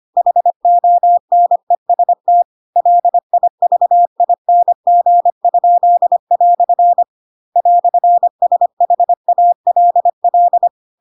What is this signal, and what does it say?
Morse code, heard as a keyed tone, about 25 wpm, 700 Hz. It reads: HONEST LIVING?" "SHALL
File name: Pinocchio / Pinocchio ch4 25 25 ik1hge